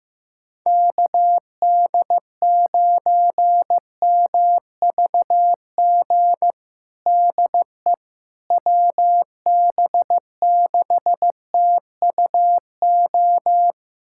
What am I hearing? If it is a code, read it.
Morse code, heard as a keyed tone, about 15 wpm, 700 Hz. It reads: KD9MVG DE WB6TUO
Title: USA Callsign DE Callsign 15wpm